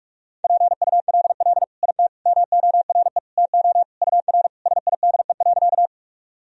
Morse code, heard as a keyed tone, about 46 wpm, 700 Hz. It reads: PULL IT MORE TO UR SIDE.
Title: Sentences from Top 400 Words and Abbreviations 45wpm